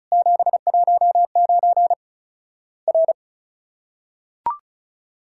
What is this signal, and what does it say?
Morse code, heard as a keyed tone, about 34 wpm, 700 Hz. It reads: 719 R
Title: Worked All Europe DX Contest - QTC Exchange - 35wpm